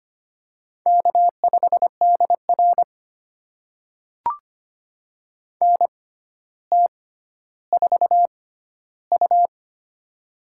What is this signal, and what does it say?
Morse code, heard as a keyed tone, about 25 wpm, spaced as for 13 wpm, 700 Hz. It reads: K5DR NT4U
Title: USA Call Signs - 4 Characters - 6x Character Spacing - 25wpm